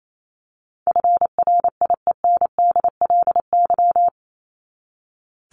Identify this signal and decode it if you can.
Morse code, heard as a keyed tone, about 28 wpm, 700 Hz. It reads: FRIENDLY
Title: Top 1751-2000 Words 28wpm